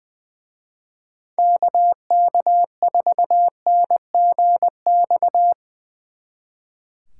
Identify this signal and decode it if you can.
Morse code, heard as a keyed tone, about 22 wpm, 700 Hz. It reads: KK4NGX
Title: Speed Racing - USA 2x3 Call Signs 20wpm